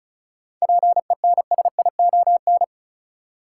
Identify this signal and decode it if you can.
Morse code, heard as a keyed tone, about 36 wpm, 700 Hz. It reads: PENSION
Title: Seven Letter Words - 35wpm